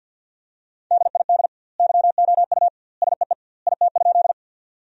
Morse code, heard as a keyed tone, about 50 wpm, 700 Hz. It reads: DID YOU SEE IT?
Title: 2x Word-Spacing - Sentences from Top 100 Words - 50wpm